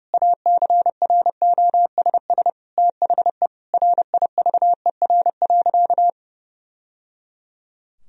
Morse code, heard as a keyed tone, about 31 wpm, 700 Hz. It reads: ACROSS THE RIVER.
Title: Sentences from Top 1000 Words - Review Part 2 - 30wpm